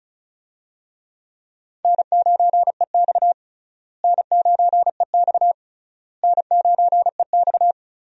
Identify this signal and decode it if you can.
Morse code, heard as a keyed tone, about 34 wpm, 700 Hz. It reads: N9EX N9EX N9EX
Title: USA Call Signs - 4 Characters - Repeated 3x - 35wpm